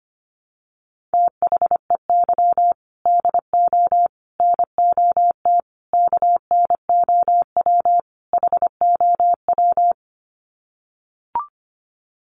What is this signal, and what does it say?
Morse code, heard as a keyed tone, about 25 wpm, 700 Hz. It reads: THEY DO NOT KNOW HOW
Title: Sets of 5 Words Encore2 25wpm